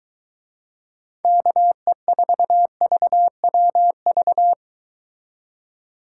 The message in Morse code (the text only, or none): KE4VWV